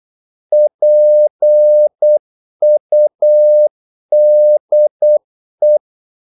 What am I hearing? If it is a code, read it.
Morse code, heard as a keyed tone, about 8 wpm, 600 Hz. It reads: PUDE
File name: Korte ord DK 8wpm